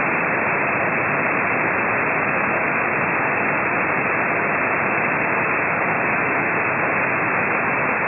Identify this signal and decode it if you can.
FT8.
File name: ft8-vs-cw / 20-to-150 FT8 -18